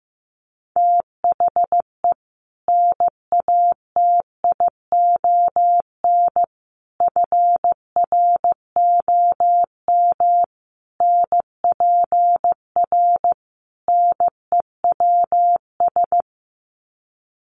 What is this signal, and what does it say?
Morse code, heard as a keyed tone, about 15 wpm, 700 Hz. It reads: THE NATION FROM NPR NEWS
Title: Sets of 5 Words 15wpm